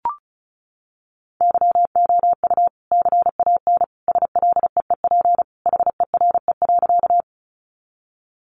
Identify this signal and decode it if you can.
Morse code, heard as a keyed tone, about 34 wpm, 700 Hz. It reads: YOU CAN SLEEP HERE.
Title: Sentences from Top 500 Words 35wpm